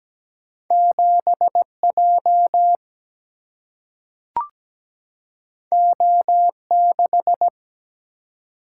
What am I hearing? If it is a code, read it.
Morse code, heard as a keyed tone, about 17 wpm, 700 Hz. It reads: 7J O6
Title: Single-Letter Single-Number 17wpm